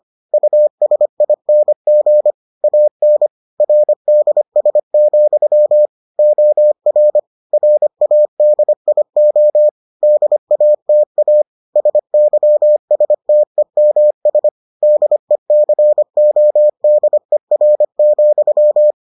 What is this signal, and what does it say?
Morse code, heard as a keyed tone, about 25 wpm, 600 Hz. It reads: USING AN RDS, OR RADIO DATA SYSTEMS DECODER,